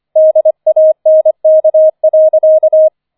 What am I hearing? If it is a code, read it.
Morse code, heard as a keyed tone, about 25 wpm, 630 Hz. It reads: DANK.